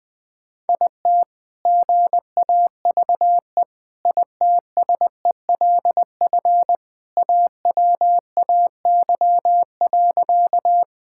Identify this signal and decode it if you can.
Morse code, heard as a keyed tone, about 20 wpm, 700 Hz. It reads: IT GAVE ITSELF AWAY.
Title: Sentences from Top 1000 Words - Review Part 5 - 20wpm